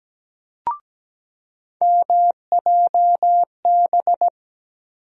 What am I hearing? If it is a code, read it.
Morse code, heard as a keyed tone, about 17 wpm, 700 Hz. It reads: MJB
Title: USA Call Sign Postfixes - 3 Letters 17wpm